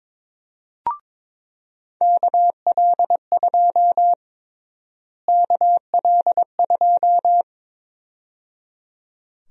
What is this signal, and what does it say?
Morse code, heard as a keyed tone, about 22 wpm, 700 Hz. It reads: KL2 KL2
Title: USA Call Sign Prefixes - Repeated 2x - 22wpm